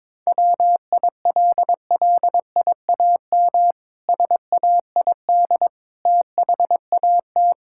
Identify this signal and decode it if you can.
Morse code, heard as a keyed tone, about 22 wpm, 700 Hz. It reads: WILLIAM SAID THAT
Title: Pangram Challenge v2 - 22wpm